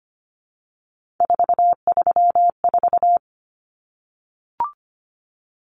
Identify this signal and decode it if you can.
Morse code, heard as a keyed tone, about 25 wpm, 700 Hz. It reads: 434
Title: Common Cut Numbers - 3-Digit 25wpm